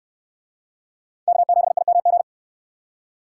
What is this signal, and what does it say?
Morse code, heard as a keyed tone, about 69 wpm, 690 Hz. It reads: K7IMG